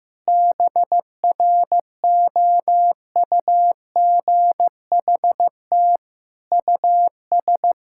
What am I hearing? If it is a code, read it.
Morse code, heard as a keyed tone, about 15 wpm, 700 Hz. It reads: BROUGHT US
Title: Sentences from Top 1000 Words - Review Part 6 - 15wpm